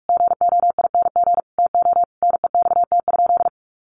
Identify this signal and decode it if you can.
Morse code, heard as a keyed tone, about 44 wpm, 700 Hz. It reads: GOING TO NEXT?
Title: Sentences from Top 1200 Words 45wpm